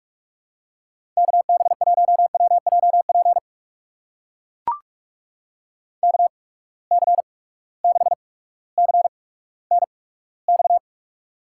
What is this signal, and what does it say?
Morse code, heard as a keyed tone, about 43 wpm, spaced as for 19 wpm, 700 Hz. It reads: KB1WJP KC6CNX
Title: USA Call Signs - 6 Characters - 8x Character Spacing - 45wpm